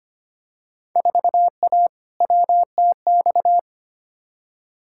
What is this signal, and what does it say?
Morse code, heard as a keyed tone, about 25 wpm, 700 Hz. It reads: 4A WTX
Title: Field Day Exchanges Encore - 25wpm